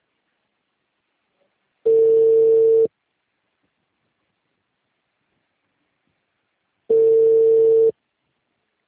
Ringback tone (China, 450 Hz).